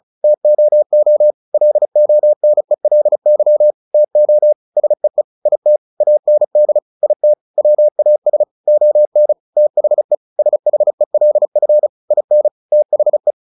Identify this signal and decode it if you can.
Morse code, heard as a keyed tone, about 34 wpm, 600 Hz. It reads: TOO LONELY TO SEE IT AND IT WAS ON THE SHELF IN THE
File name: the old man and the sea / the old man and the sea 1 35wpm